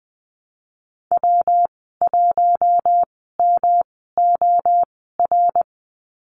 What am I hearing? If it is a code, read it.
Morse code, heard as a keyed tone, about 20 wpm, spaced as for 15 wpm, 700 Hz. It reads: W1MOR
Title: 147 Lesson 24g - Letter y - Call Signs 20wpm